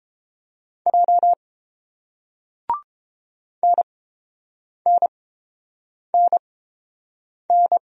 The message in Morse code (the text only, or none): J NNNN